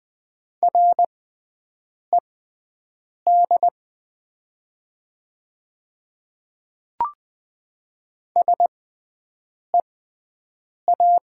Morse code, heard as a keyed tone, about 20 wpm, spaced as for 7 wpm, 700 Hz. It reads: RED SEA
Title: 6x Character Spacing - Top 500 Words - 20wpm